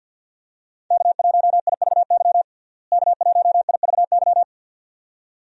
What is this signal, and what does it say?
Morse code, heard as a keyed tone, about 53 wpm, 700 Hz. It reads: K1IVY K1IVY
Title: USA Call Signs - 5 Characters - Repeated 2x - 50wpm